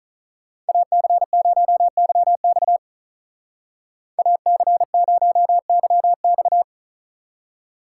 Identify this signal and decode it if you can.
Morse code, heard as a keyed tone, about 43 wpm, 700 Hz. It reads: AC0YX AC0YX
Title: Speed Racing - USA 2x2 Call Signs 35wpm